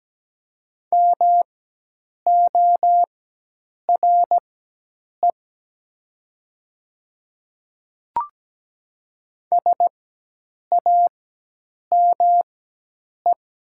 Morse code, heard as a keyed tone, about 17 wpm, spaced as for 8 wpm, 700 Hz. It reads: MORE SAME
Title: 4x Character Spacing - Top 200 Words - 17wpm